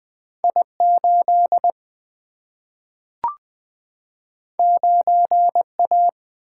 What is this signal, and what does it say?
Morse code, heard as a keyed tone, about 20 wpm, 700 Hz. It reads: I 8 9 A
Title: Single-Letter Single-Number 20wpm